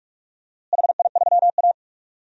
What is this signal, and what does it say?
Morse code, heard as a keyed tone, about 46 wpm, 700 Hz. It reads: HI3A
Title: DXCC-100 Call Signs - 4 Characters - Repeated 2x - 45wpm